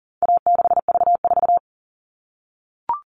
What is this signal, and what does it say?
Morse code, heard as a keyed tone, about 40 wpm, 700 Hz. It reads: A6V4
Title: All Cut Numbers - 4-Digit 40wpm